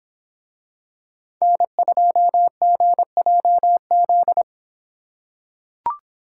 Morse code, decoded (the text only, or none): N2GJZ